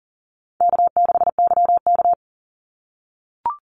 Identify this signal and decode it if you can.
Morse code, heard as a keyed tone, about 40 wpm, 700 Hz. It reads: K6YK
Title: NA SOTA Chaser Callsigns 40wpm